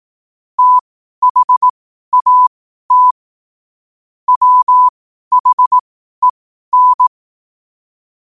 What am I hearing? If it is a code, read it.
Morse code, heard as a keyed tone, about 18 wpm, spaced as for 12 wpm, 1000 Hz. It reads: THAT WHEN
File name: qotdfiles / 12wpm